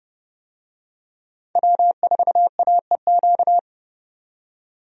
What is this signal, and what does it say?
Morse code, heard as a keyed tone, about 29 wpm, 700 Hz. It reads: W4AEQ
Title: USA Call Signs - 5 Characters - 30wpm